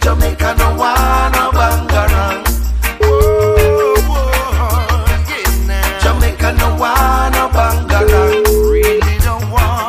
Ringback tone (Europe, 425 Hz), with noise.